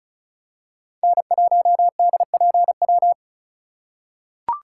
Morse code, heard as a keyed tone, about 34 wpm, 700 Hz. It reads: N1DPW
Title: USA Call Signs - 8x Character Spacing - 35wpm